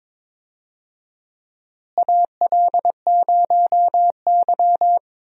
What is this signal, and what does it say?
Morse code, heard as a keyed tone, about 22 wpm, 700 Hz. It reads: AL0Y
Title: USA Call Signs - 4 Characters - Repeated 3x - 22wpm